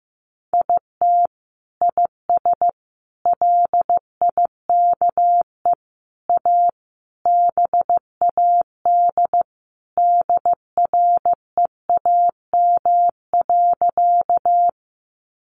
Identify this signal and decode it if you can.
Morse code, heard as a keyed tone, about 15 wpm, 700 Hz. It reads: IT IS LIKE A BAD DREAM.
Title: Sentences from Top 1500 Words 15wpm